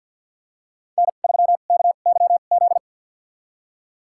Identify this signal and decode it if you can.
Morse code, heard as a keyed tone, about 47 wpm, spaced as for 35 wpm, 700 Hz. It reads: N3KYZ
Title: USA Call Signs - 5 Characters - 2x Character Spacing - 50wpm